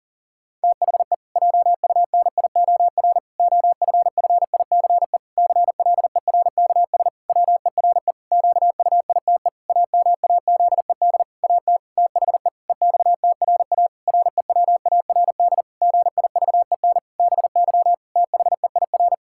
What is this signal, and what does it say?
Morse code, heard as a keyed tone, about 40 wpm, 700 Hz. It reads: THE JUNIOR OFFICE CLERKS WERE QUITE AMAZED AT THE EXTRA REWARD GIVEN BY THEIR
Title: Pangram Challenge v2 - 40wpm